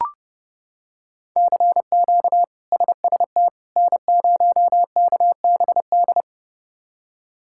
Morse code, heard as a keyed tone, about 30 wpm, 700 Hz. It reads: CQ SST N0KBD
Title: K1USN SST QSO Practice - 30wpm